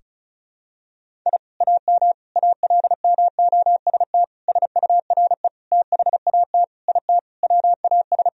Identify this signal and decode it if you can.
Morse code, heard as a keyed tone, about 36 wpm, 700 Hz. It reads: I AM ALMOST SURE THAT IT WAS
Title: Sentences from Top 1000 Words - Review Part 4 - 35wpm